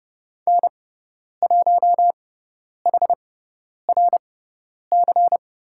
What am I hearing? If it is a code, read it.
Morse code, heard as a keyed tone, about 30 wpm, spaced as for 10 wpm, 700 Hz. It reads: N1HRC
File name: Lesson 16g - Letter c - Call Signs 30wpm